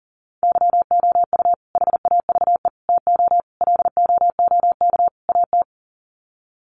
Morse code, heard as a keyed tone, about 40 wpm, 700 Hz. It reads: YOU HAVE TO LOOK AT
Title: Sets of 5 Words Encore 40wpm